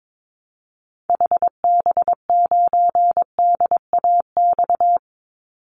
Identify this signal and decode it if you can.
Morse code, heard as a keyed tone, about 22 wpm, 700 Hz. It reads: HB9DAX
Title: Call Signs Repeated 2x - 22wpm